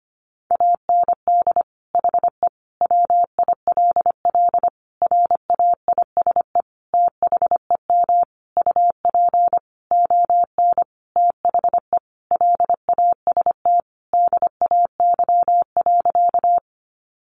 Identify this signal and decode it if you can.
Morse code, heard as a keyed tone, about 25 wpm, 700 Hz. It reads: AND HE WILL RAISE THEM UP ON THE LAST DAY.
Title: Sentences from Top 1400 Words 25wpm